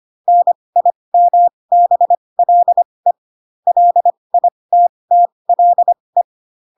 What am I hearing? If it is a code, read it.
Morse code, heard as a keyed tone, about 25 wpm, spaced as for 20 wpm, 700 Hz. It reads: NIMBLE LITTLE
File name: Pinocchio / Pinocchio ch8 15 25 ik1hge